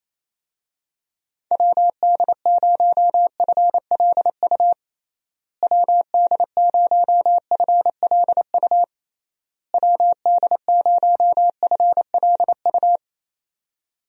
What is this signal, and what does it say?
Morse code, heard as a keyed tone, about 28 wpm, 700 Hz. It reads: WD0FLU WD0FLU WD0FLU
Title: USA Call Signs - 6 Characters - Repeated 3x - 28wpm